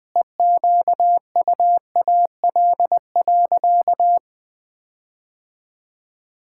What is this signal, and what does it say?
Morse code, heard as a keyed tone, about 20 wpm, 700 Hz. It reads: EQUAL.
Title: Sentences with 9-Word Phrases - 20wpm